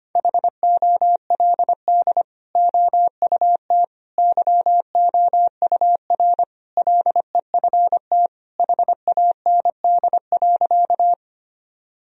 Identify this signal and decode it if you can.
Morse code, heard as a keyed tone, about 25 wpm, 700 Hz. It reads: HOLD OUT YOUR LEFT HAND.
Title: Sentences from Top 1000 Words - Review Part 2 - 25wpm